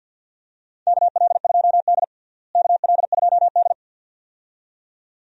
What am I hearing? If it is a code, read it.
Morse code, heard as a keyed tone, about 45 wpm, 700 Hz. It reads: KL2D KL2D
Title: USA Call Signs - 4 Characters - Repeated 2x - 50wpm